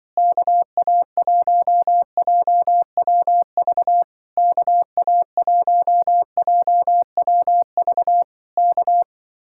Morse code, heard as a keyed tone, about 24 wpm, 700 Hz. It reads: KA1JWV KA1JWV K
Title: Speed-Racing Common QSO Phrases Encore - 20wpm